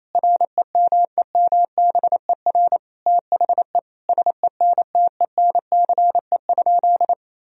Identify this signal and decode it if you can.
Morse code, heard as a keyed tone, about 27 wpm, 700 Hz. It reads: REMEMBER THE SENTENCE?
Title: Sentences from Top 1000 Words - Review Part 1 - 28wpm